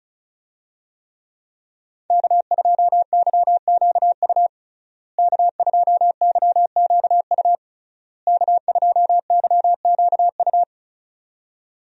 Morse code, heard as a keyed tone, about 36 wpm, 700 Hz. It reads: K2YQU K2YQU K2YQU
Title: USA Call Signs - 5 Characters - Repeated 3x - 35wpm